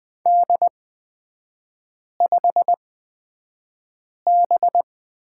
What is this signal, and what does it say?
Morse code, heard as a keyed tone, about 20 wpm, spaced as for 5 wpm, 700 Hz. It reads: D5B
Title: 243 Lesson 39e - Letter z - Three Characters 20wpm